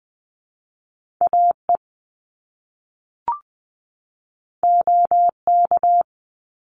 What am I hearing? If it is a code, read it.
Morse code, heard as a keyed tone, about 20 wpm, 700 Hz. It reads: AE OK
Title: Top Two Letter Combinations in Words 20wpm